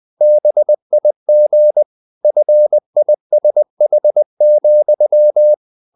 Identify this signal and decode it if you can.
Morse code, heard as a keyed tone, about 20 wpm, 600 Hz. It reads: BIG FISH,
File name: the old man and the sea / the old man and the sea part 1 20wpm